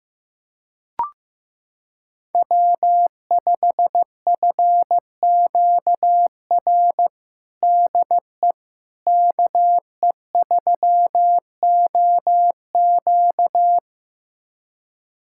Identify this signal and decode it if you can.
Morse code, heard as a keyed tone, about 16 wpm, 700 Hz. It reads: W5FQR DE KE3OQ
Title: USA Callsign DE Callsign - 5 Characters - 15wpm